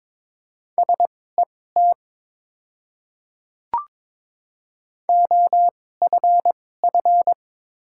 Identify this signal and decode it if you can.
Morse code, heard as a keyed tone, about 22 wpm, spaced as for 16 wpm, 700 Hz. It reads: SET OFF
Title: 2x Character Spacing - Top 200 Words - 22wpm